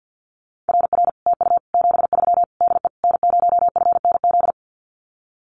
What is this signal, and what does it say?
Morse code, heard as a keyed tone, about 50 wpm, 700 Hz. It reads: RR TU 73 DE N0FNZ
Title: Common QSO Phrases Encore 50wpm